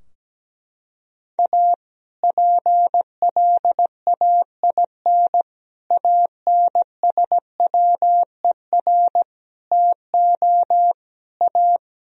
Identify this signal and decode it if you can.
Morse code, heard as a keyed tone, about 17 wpm, 700 Hz. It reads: A PLAIN ANSWER TO A
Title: Sentences from Top 1000 Words - Review Part 2 - 17wpm